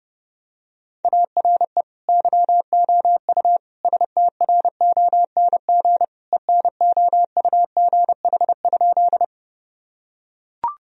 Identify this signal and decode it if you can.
Morse code, heard as a keyed tone, about 32 wpm, 700 Hz. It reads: ARE YOU STRONG ENOUGH?
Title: Sentences from Top 1000 Words - Review Part 3 - 30wpm